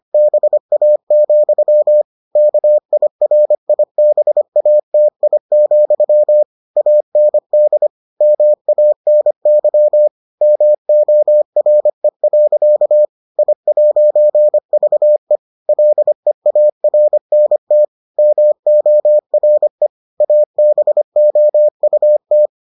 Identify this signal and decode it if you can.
Morse code, heard as a keyed tone, about 25 wpm, 600 Hz. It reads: BA, KIRIBATI, AND MANY MORE. I'VE LEARNT MORE ABOUT